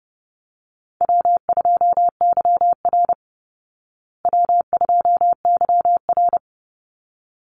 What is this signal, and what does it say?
Morse code, heard as a keyed tone, about 30 wpm, 700 Hz. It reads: W2YR W2YR
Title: Easy Call Signs Repeated 2x - 30wpm